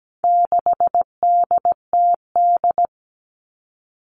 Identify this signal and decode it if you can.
Morse code, heard as a keyed tone, about 17 wpm, 700 Hz. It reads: 6DTD
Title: All Cut Numbers - 4-Digit 17wpm